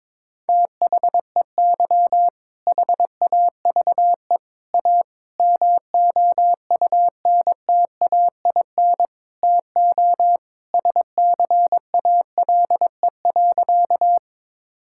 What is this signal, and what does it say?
Morse code, heard as a keyed tone, about 22 wpm, 700 Hz. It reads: THEY HAVE A MOUNTAIN TO SCALE.
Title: Sentences from Top 1000 Words - Review Part 5 - 22wpm